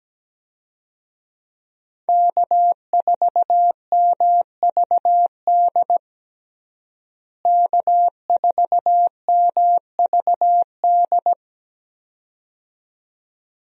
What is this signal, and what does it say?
Morse code, heard as a keyed tone, about 17 wpm, 700 Hz. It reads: K4MVD K4MVD
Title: USA Call Signs - 5 Characters - Repeated 2x - 17wpm